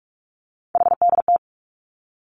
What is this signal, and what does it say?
Morse code, heard as a keyed tone, about 46 wpm, 700 Hz. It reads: HDT
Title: Three Challenging Letter-Number Combinations 45wpm